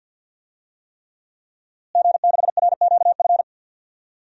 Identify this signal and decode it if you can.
Morse code, heard as a keyed tone, about 52 wpm, 700 Hz. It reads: G6RQF
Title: DXCC-100 Call Signs - 5 Characters - Repeated 3x - 50wpm